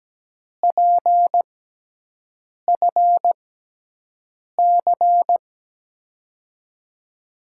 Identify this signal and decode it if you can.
Morse code, heard as a keyed tone, about 17 wpm, spaced as for 6 wpm, 700 Hz. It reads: PFC
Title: USA Call Signs - 6 Characters - 6x Character Spacing - 17wpm